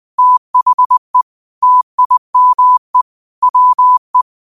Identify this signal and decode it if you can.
Morse code, heard as a keyed tone, about 20 wpm, 1000 Hz. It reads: THE TIME WE
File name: qotdfiles / 20wpm